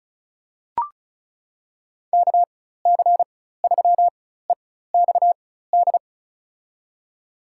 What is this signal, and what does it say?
Morse code, heard as a keyed tone, about 36 wpm, spaced as for 16 wpm, 700 Hz. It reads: KC3EXD
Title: USA Call Signs - 4x Character Spacing - 35wpm